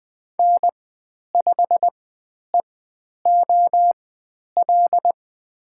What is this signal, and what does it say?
Morse code, heard as a keyed tone, about 20 wpm, spaced as for 10 wpm, 700 Hz. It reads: N5EOL
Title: 109 Lesson 18g - Letter m - Call Signs 20wpm